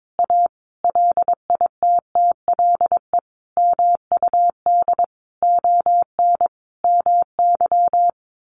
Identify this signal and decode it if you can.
Morse code, heard as a keyed tone, about 22 wpm, 700 Hz. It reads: A LITTLE MUD ON MY
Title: A Sound Of Thunder 22wpm